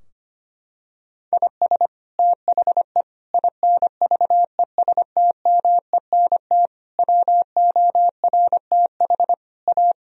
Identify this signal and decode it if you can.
Morse code, heard as a keyed tone, about 25 wpm, 700 Hz. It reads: IS THE INVESTMENT WORTH A
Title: Sentences from Top 1000 Words - Review Part 4 - 25wpm